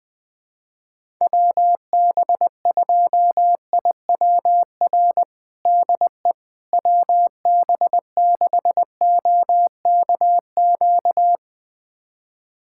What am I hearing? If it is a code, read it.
Morse code, heard as a keyed tone, about 20 wpm, 700 Hz. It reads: WB2IWR DE WB6OKQ
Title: USA Callsign DE Callsign 20wpm